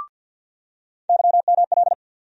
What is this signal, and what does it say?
Morse code, heard as a keyed tone, about 48 wpm, 700 Hz. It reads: YML
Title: USA Call Sign Postfixes - 3 Letters 50wpm